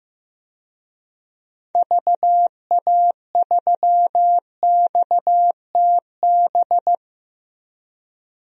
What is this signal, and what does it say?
Morse code, heard as a keyed tone, about 15 wpm, 700 Hz. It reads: VA3XTB
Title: DXCC-100 Call Signs - 6 Characters - Repeated 3x - 15wpm